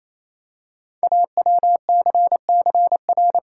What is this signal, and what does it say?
Morse code, heard as a keyed tone, about 27 wpm, 700 Hz. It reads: AWCCR
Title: Code Groups - 5 Letters - 28wpm